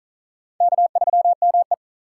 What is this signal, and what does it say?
Morse code, heard as a keyed tone, about 47 wpm, 700 Hz. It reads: K3ME